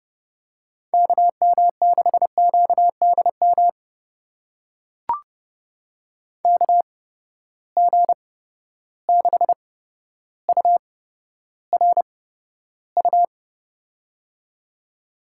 Morse code, heard as a keyed tone, about 30 wpm, spaced as for 13 wpm, 700 Hz. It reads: KM6QDM KG6URU
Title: USA Call Signs - 6 Characters - 8x Character Spacing - 30wpm